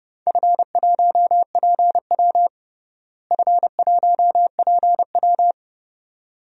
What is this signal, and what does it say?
Morse code, heard as a keyed tone, about 30 wpm, 700 Hz. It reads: F1PW F1PW
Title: DXCC-100 Call Signs - 4 Characters - Repeated 2x - 30wpm